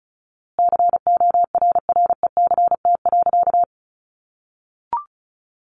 Morse code, heard as a keyed tone, about 34 wpm, 700 Hz. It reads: CORRECT.
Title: A Sound Of Thunder 35wpm